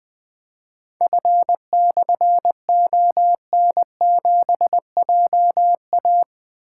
Morse code, heard as a keyed tone, about 20 wpm, 700 Hz. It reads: F/ON7JA
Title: Difficult Call Signs 20wpm